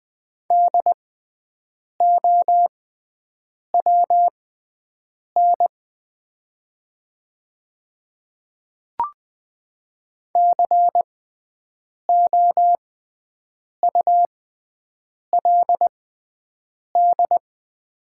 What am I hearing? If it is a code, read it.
Morse code, heard as a keyed tone, about 20 wpm, spaced as for 7 wpm, 700 Hz. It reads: DOWN COULD
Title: 6x Character Spacing - Top 100 Words - 20wpm